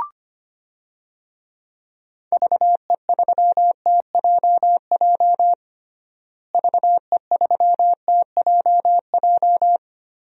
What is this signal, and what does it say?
Morse code, heard as a keyed tone, about 25 wpm, 700 Hz. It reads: VE3TJJ VE3TJJ
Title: DXCC-100 Call Signs - 6 Characters - Repeated 3x - 25wpm